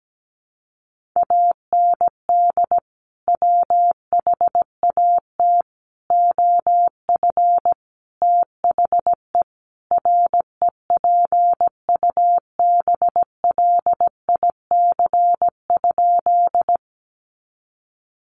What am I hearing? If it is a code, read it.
Morse code, heard as a keyed tone, about 17 wpm, 700 Hz. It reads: AND WHAT OF THE REPUBLIC?
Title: Sentences from Top 1500 Words 17wpm